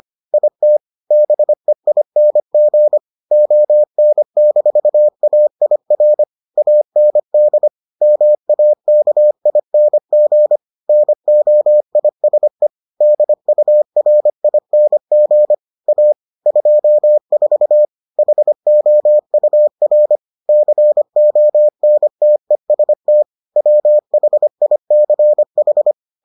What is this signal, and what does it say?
Morse code, heard as a keyed tone, about 25 wpm, 600 Hz. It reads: IT BEING ON-AIR AND MAKING NOISE DURING A 24 HOUR CONTEST WHICH